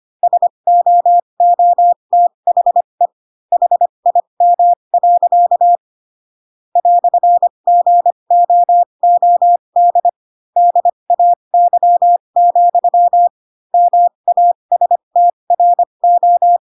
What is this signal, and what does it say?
Morse code, heard as a keyed tone, about 25 wpm, spaced as for 22 wpm, 700 Hz. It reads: SOOTHE HIM. "GOOD DAY, MASTRO
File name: Pinocchio / Pinocchio ch2 18 25 ik1hge